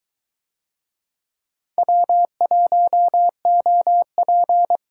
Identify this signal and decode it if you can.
Morse code, heard as a keyed tone, about 20 wpm, 700 Hz. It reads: W1OP